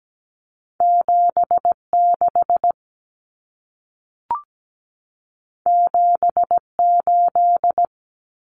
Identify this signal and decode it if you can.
Morse code, heard as a keyed tone, about 17 wpm, 700 Hz. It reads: 76 78
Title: Numbers - 2-Digit 17wpm